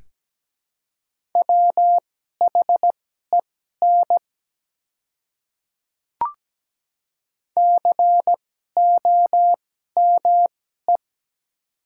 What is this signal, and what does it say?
Morse code, heard as a keyed tone, about 17 wpm, spaced as for 12 wpm, 700 Hz. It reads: WHEN COME
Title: 2x Character Spacing - Top 100 Words - 17wpm